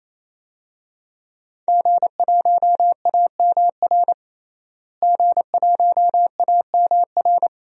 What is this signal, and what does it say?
Morse code, heard as a keyed tone, about 28 wpm, 690 Hz. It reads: G1AMR G1AMR
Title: DXCC-100 Call Signs - 5 Characters - Repeated 3x - 28wpm